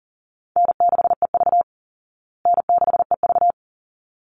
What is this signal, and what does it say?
Morse code, heard as a keyed tone, about 37 wpm, 700 Hz. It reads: N6EV N6EV
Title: Easy Call Signs Repeated 2x - 40wpm